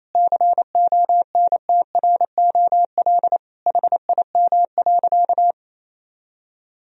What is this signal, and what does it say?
Morse code, heard as a keyed tone, about 28 wpm, 700 Hz. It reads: CONTROL HIM.
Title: Sentences from Top 1000 Words - Review Part 3 - 28wpm